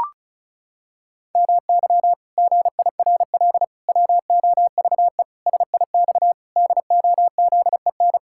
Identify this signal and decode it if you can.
Morse code, heard as a keyed tone, about 34 wpm, 700 Hz. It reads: MY GIRL WOVE SIX DOZEN
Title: Pangram Challenge v2 - 35wpm